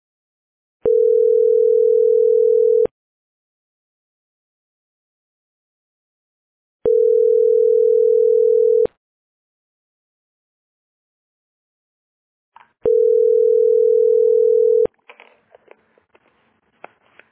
Ringback tone (North America), 440 + 480 Hz.